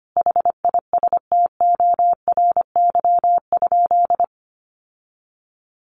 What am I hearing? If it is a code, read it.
Morse code, heard as a keyed tone, about 25 wpm, 700 Hz. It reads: HISTORY?
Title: Sentences from Top 1500 Words 25wpm